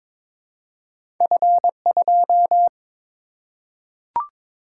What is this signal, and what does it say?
Morse code, heard as a keyed tone, about 22 wpm, 690 Hz. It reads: F2
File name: practice / 22wpm / Two Letter-Number Combinations - 22wpm